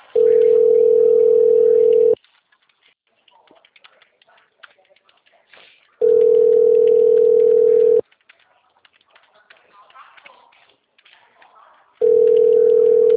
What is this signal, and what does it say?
Ringback tone (North America), 440 + 480 Hz, with speech.
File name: out- -2025